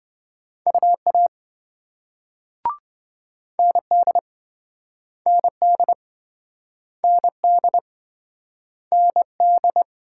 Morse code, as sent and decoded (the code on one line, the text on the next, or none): ..- .- / -. -.. / -. -.. / -. -.. / -. -..
UA ND ND ND ND